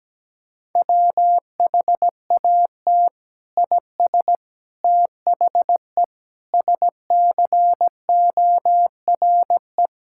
Morse code, heard as a keyed tone, about 17 wpm, 700 Hz. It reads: WHAT IS THE SCORE
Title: Sentences from Top 1600 Words 17wpm